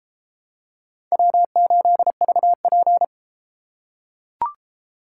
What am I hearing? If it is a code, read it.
Morse code, heard as a keyed tone, about 32 wpm, 700 Hz. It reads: W8VP